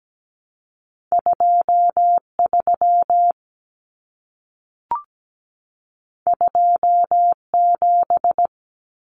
Morse code, heard as a keyed tone, about 17 wpm, 700 Hz. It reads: 23 27
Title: Common Cut Numbers - 2-Digit 17wpm